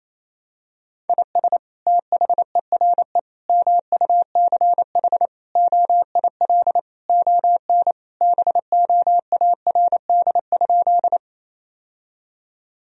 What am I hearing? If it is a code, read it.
Morse code, heard as a keyed tone, about 28 wpm, 700 Hz. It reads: IS THERE MUCH OIL ON BOARD?
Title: Sentences from Top 1000 Words - Review Part 4 - 28wpm